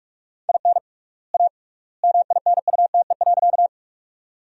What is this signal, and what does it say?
Morse code, heard as a keyed tone, about 42 wpm, 700 Hz. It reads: IN A MINUTE.
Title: 3x Word-Spacing - Sentences from Top 500 Words - 45wpm